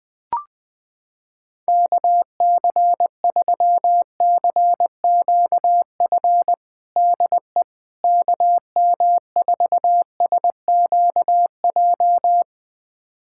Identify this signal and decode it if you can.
Morse code, heard as a keyed tone, about 20 wpm, 700 Hz. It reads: KC3CQF DE KM4SQJ
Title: USA Callsign DE Callsign - 6 Characters - 20wpm